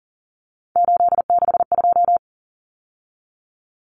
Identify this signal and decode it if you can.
Morse code, heard as a keyed tone, about 40 wpm, 700 Hz. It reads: 862
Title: Numbers - 3-Digit 40wpm